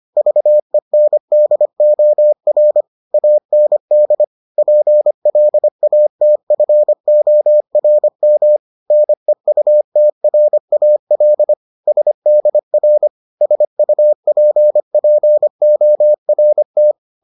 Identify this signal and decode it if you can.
Morse code, heard as a keyed tone, about 25 wpm, 600 Hz. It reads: VENDOR AND PLATFORM NEUTRAL SDR SUPPORT